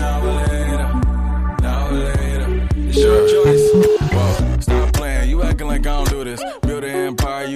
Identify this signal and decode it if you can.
Ringback tone (Europe, 425 Hz), with noise.